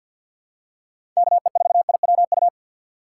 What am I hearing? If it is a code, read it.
Morse code, heard as a keyed tone, about 51 wpm, 700 Hz. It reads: KE4IWU
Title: USA Call Signs - 6 Characters - 4x Character Spacing - 50wpm